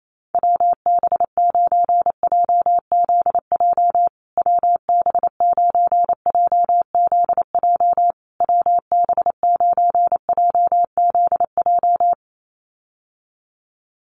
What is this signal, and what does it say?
Morse code, heard as a keyed tone, about 28 wpm, 700 Hz. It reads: WB9JZJ WB9JZJ WB9JZJ
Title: USA Call Signs Repeated 3x 28wpm